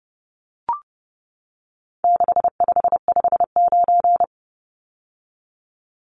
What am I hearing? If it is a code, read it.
Morse code, heard as a keyed tone, about 30 wpm, 700 Hz. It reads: 6559
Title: Numbers - 4-Digit 30wpm